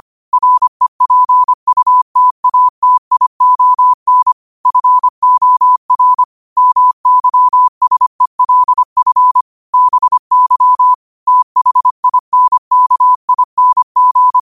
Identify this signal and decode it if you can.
Morse code, heard as a keyed tone, about 25 wpm, 1000 Hz. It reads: REPUTATION FOR MYSELF BY THINKING